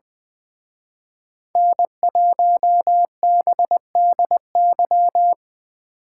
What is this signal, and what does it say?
Morse code, heard as a keyed tone, about 24 wpm, 700 Hz. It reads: N1BDY
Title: Speed Racing - USA Call Signs 20wpm